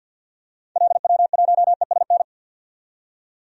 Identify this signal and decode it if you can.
Morse code, heard as a keyed tone, about 48 wpm, 700 Hz. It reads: LW1ESN
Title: DXCC-100 Call Signs - 6 Characters - 50wpm